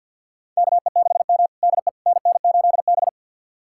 Morse code, heard as a keyed tone, about 52 wpm, 700 Hz. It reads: KE6M DE NN8B
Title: USA Callsign DE Callsign - 4 Characters - 50wpm